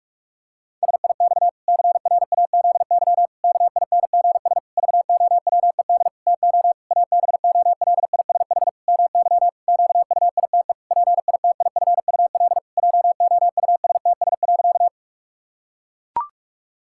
Morse code, heard as a keyed tone, about 41 wpm, 700 Hz. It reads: SIX CRAZY KINGS VOWED TO ABOLISH MY QUITE PITIFUL JOUSTS.
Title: Pangram Challenge v2 - 45wpm